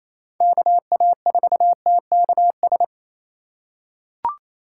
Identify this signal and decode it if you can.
Morse code, heard as a keyed tone, about 28 wpm, 700 Hz. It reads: KA4TKS
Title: USA Call Signs - 6 Characters - 2x Character Spacing - 28wpm